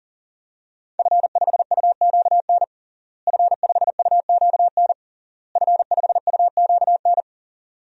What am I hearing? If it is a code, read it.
Morse code, heard as a keyed tone, about 40 wpm, 700 Hz. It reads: F5UQN F5UQN F5UQN
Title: DXCC-100 Call Signs - 5 Characters - Repeated 3x - 40wpm